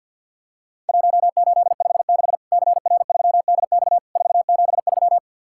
Morse code, heard as a keyed tone, about 50 wpm, 700 Hz. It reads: 1856 CR3DX 473
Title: Worked All Europe DX Contest - QTC Exchange - 50wpm